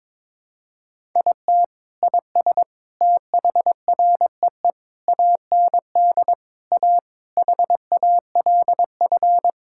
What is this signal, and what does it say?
Morse code, heard as a keyed tone, about 22 wpm, 700 Hz. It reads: IT IS THREE AND A HALF
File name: practice / 22wpm / Sentences with 3-Word Phrases - 22wpm